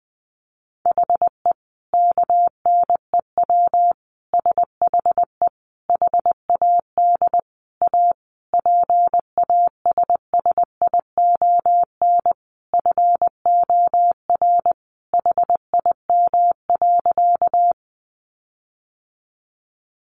Morse code, heard as a keyed tone, about 20 wpm, 700 Hz. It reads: HE KNEW SHE HAD A PASSION FOR HIM.
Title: Sentences from Top 1300 Words 20wpm